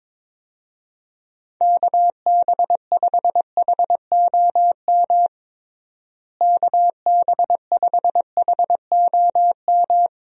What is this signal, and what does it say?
Morse code, heard as a keyed tone, about 22 wpm, 700 Hz. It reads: KB5HOM KB5HOM
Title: USA Call Signs - 6 Characters - Repeated 3x - 22wpm